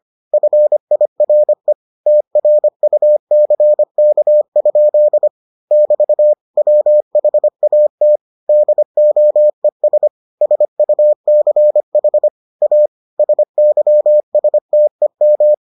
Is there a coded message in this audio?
Yes: Morse code from 0.3 to 15.5 s, heard as a keyed tone, about 25 wpm, 600 Hz. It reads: FIRE TRUCK? <BT> WHAT DOES SUCH A SYSTEM